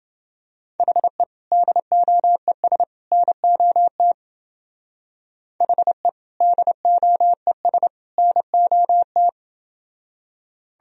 Morse code, heard as a keyed tone, about 28 wpm, 700 Hz. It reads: HE DOES NOT HE DOES NOT